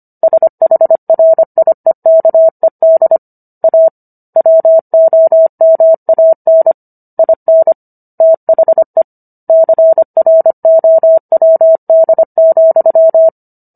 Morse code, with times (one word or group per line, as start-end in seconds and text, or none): 0.2-3.2 SHRIEKED
3.6-3.9 A
4.4-6.7 WOMAN
7.2-7.7 IN
8.2-9.0 THE
9.5-13.3 CROWD,